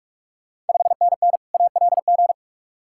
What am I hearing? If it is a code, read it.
Morse code, heard as a keyed tone, about 46 wpm, 700 Hz. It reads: 5NN ALG
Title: Ontario QSO Party Exchange 45wpm